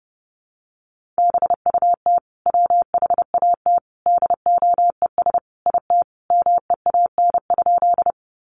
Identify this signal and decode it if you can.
Morse code, heard as a keyed tone, about 30 wpm, 700 Hz. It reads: BUT WHAT DOES IT MEAN?
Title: Sentences from Top 200 Words 30wpm